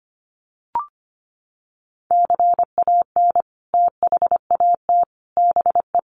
Morse code, heard as a keyed tone, about 25 wpm, 700 Hz. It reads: CAN THAT BE
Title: Sentences from Top 1200 Words 25wpm